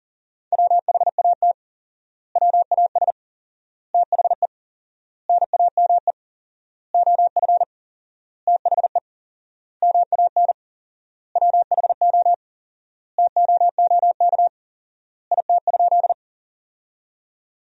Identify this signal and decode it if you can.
Morse code, heard as a keyed tone, about 37 wpm, 700 Hz. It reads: WHAT WAS THE NAME OF THE MAN WHO TOOK IT?
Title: 4x Word-Spacing - Sentences from Top 300 Words - 40wpm